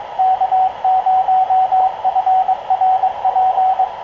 Morse code, heard as a keyed tone, about 23 wpm, 750 Hz, with radio noise. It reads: K0FRP
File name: k0frp ki0ii wa8bxn fox utc